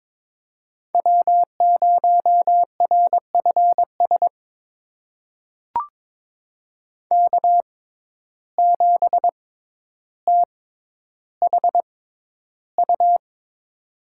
Morse code, heard as a keyed tone, about 22 wpm, spaced as for 11 wpm, 700 Hz. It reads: W0RFS K7THU
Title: USA Call Signs - 5 Characters - 6x Character Spacing - 22wpm